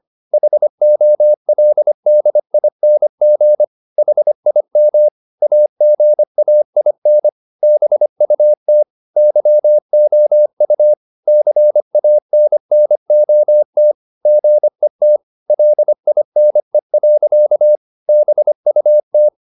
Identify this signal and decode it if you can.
Morse code, heard as a keyed tone, about 25 wpm, 600 Hz. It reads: HOLDING HIM AGAIN BUT YOU CANNOT GET LINE. BUT